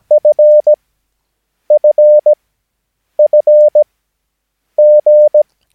Morse code, heard as a keyed tone, about 17 wpm, spaced as for 7 wpm, 600 Hz. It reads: FFFG
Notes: Övningsmaterial med alfabetet A till Ö samt siffror 0 till 9 enligt International/Continental Morse Code.